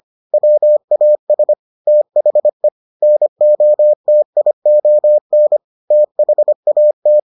Morse code, heard as a keyed tone, about 25 wpm, 600 Hz. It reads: WAS THE NOTION THAT